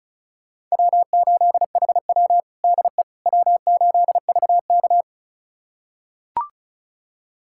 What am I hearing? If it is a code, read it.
Morse code, heard as a keyed tone, about 36 wpm, 700 Hz. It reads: W8HW DE W8VK
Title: USA Callsign DE Callsign - 4 Characters - 35wpm